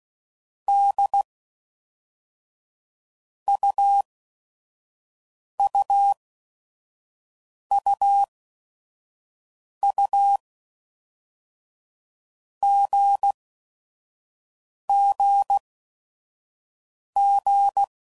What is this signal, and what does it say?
Morse code, heard as a keyed tone, about 16 wpm, 800 Hz. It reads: D U U U U G G G
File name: Basic Letter Groups By Blocks